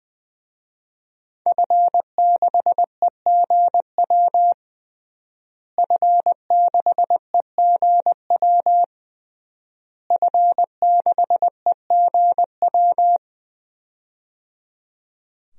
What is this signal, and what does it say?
Morse code, heard as a keyed tone, about 20 wpm, 700 Hz. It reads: F6EGW F6EGW F6EGW
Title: DXCC-100 Call Signs - 5 Characters - Repeated 3x - 20wpm